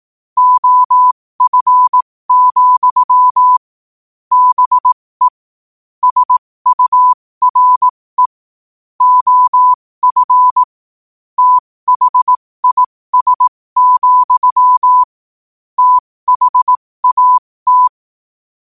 Morse code, heard as a keyed tone, about 18 wpm, spaced as for 15 wpm, 1000 Hz. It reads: OF, BE SURE OF THIS, THAT
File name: qotdfiles / 15wpm